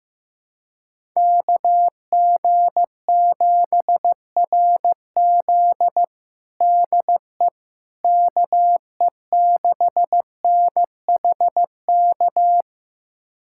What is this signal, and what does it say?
Morse code, heard as a keyed tone, about 16 wpm, 700 Hz. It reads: KG7RZ DE KE6NHK
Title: USA Callsign DE Callsign 15wpm